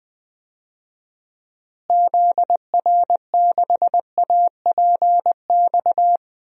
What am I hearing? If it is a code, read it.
Morse code, heard as a keyed tone, about 21 wpm, 700 Hz. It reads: ZR6APX
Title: DXCC-100 Call Signs - 6 Characters - Repeated 2x - 20wpm